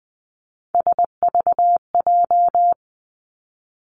Morse code, heard as a keyed tone, about 20 wpm, 700 Hz. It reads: SVJ
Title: Three Challenging Letter-Number Combinations 20wpm